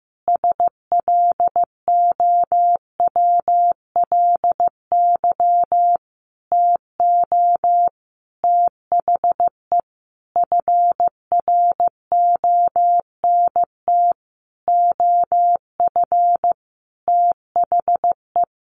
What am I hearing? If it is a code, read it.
Morse code, heard as a keyed tone, about 15 wpm, 700 Hz. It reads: SLOWLY TO THE FRONT OF THE
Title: Sentences from Top 1400 Words 15wpm